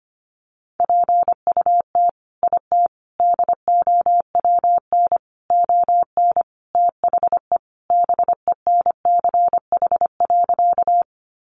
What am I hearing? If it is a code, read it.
Morse code, heard as a keyed tone, about 25 wpm, 700 Hz. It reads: PUT IT DOWN ON THE BENCH.
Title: Sentences from Top 1400 Words 25wpm